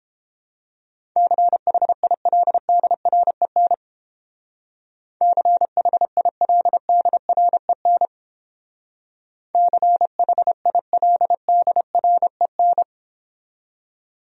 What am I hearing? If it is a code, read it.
Morse code, heard as a keyed tone, about 29 wpm, 700 Hz. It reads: CHILDREN CHILDREN CHILDREN